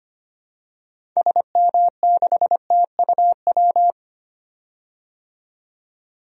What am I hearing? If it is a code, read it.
Morse code, heard as a keyed tone, about 25 wpm, 700 Hz. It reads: SM6TUW
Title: DXCC-100 Call Signs - 6 Characters - 25wpm